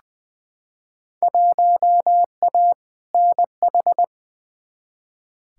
Morse code, heard as a keyed tone, about 21 wpm, 700 Hz. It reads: 1A NH
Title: Field Day Exchanges Encore - 20wpm